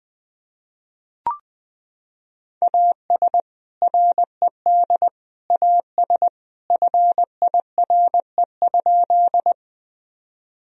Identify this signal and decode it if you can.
Morse code, heard as a keyed tone, about 21 wpm, 700 Hz. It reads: AS RED AS FIRE?
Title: Sentences from Top 1000 Words - Review Part 2 - 20wpm